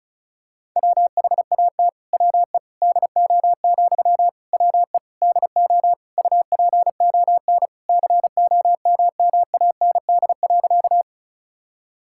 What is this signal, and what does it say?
Morse code, heard as a keyed tone, about 36 wpm, 700 Hz. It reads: WHAT WE DO, WE DO UPON COMMAND.
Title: Sentences from Top 1600 Words 35wpm